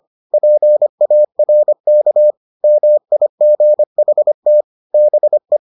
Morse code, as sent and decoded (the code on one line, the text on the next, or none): .--. .- .-. -.- / -- .. --. .... - / -... .
PARK MIGHT BE